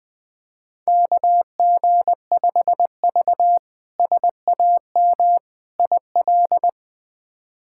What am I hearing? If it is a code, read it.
Morse code, heard as a keyed tone, about 21 wpm, 700 Hz. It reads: KG5V SAM IL
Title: K1USN SST QSO Practice - 20wpm